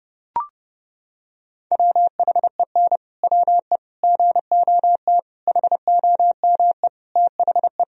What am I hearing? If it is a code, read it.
Morse code, heard as a keyed tone, about 32 wpm, 700 Hz. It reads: WHEN WE GOT HOME THE
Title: Sentences from Top 500 Words and Abbreviations 30wpm